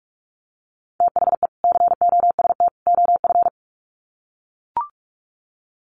Morse code, heard as a keyed tone, about 44 wpm, 700 Hz. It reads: THE COST OF
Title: Sets of 3 Words 45wpm